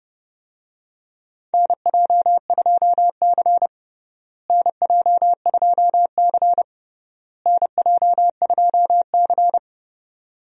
Morse code, heard as a keyed tone, about 30 wpm, 700 Hz. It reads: NJ2C NJ2C NJ2C
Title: USA Call Signs - 4 Characters - Repeated 3x - 30wpm